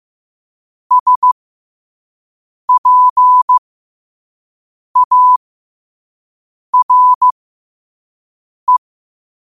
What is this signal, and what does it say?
Morse code, heard as a keyed tone, about 15 wpm, spaced as for 5 wpm, 1000 Hz. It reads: SPARE
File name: qotdfiles / 5wpm